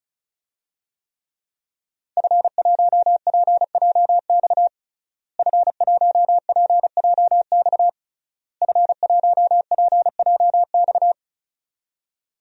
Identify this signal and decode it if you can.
Morse code, heard as a keyed tone, about 34 wpm, 700 Hz. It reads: F1PJX F1PJX F1PJX
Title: DXCC-100 Call Signs - 5 Characters - Repeated 3x - 35wpm